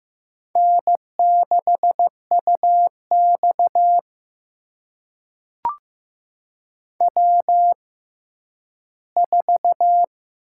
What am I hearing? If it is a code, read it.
Morse code, heard as a keyed tone, about 15 wpm, spaced as for 10 wpm, 700 Hz. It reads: N6UX W4
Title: USA Call Signs - 6x Character Spacing - 15wpm